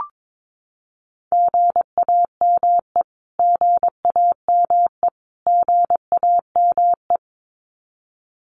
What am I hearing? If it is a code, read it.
Morse code, heard as a keyed tone, about 22 wpm, 700 Hz. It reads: GAME GAME GAME
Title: Top 500 Words Repeated 22wpm